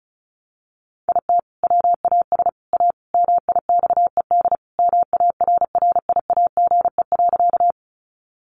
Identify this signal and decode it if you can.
Morse code, heard as a keyed tone, about 34 wpm, 700 Hz. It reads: IT WAS A MIXED MARRIAGE.
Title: Sentences from Top 1500 Words 35wpm